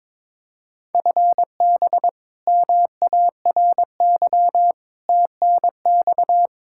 Morse code, heard as a keyed tone, about 22 wpm, 700 Hz. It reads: FB MARY TNX
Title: Practice QSO - Exchange 3 - More Info - 22wpm